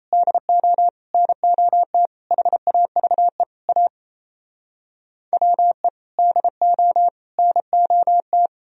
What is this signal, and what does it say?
Morse code, heard as a keyed tone, about 37 wpm, 700 Hz. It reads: DO NOT HAVE A WE DO NOT